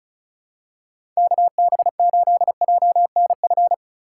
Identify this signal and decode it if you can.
Morse code, heard as a keyed tone, about 36 wpm, 700 Hz. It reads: KB8JNF
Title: USA Call Signs - 6 Characters - 3x Character Spacing - 35wpm